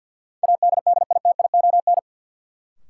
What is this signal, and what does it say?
Morse code, heard as a keyed tone, about 52 wpm, 700 Hz. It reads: ADDITION
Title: Sentences from Top 1500 Words - New Word Drill Alt - 50wpm